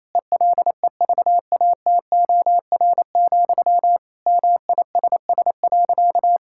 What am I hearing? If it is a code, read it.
Morse code, heard as a keyed tone, about 28 wpm, 700 Hz. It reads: ELEVATOR, MISS.
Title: Sentences from Top 1600 Words 28wpm